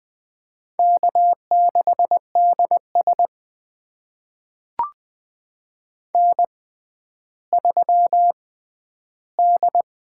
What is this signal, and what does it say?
Morse code, heard as a keyed tone, about 20 wpm, spaced as for 11 wpm, 700 Hz. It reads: K6DS N3D
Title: USA Call Signs - 6x Character Spacing - 20wpm